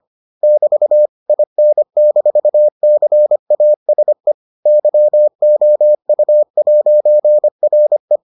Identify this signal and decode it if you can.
Morse code, heard as a keyed tone, about 25 wpm, 600 Hz. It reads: <BT> IN-CASE YOU'RE